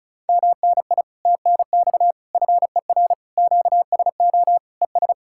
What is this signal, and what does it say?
Morse code, heard as a keyed tone, about 46 wpm, spaced as for 41 wpm, 700 Hz. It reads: MNI TNX FER QSO ES
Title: Speed-Racing Common QSO Phrases - 35wpm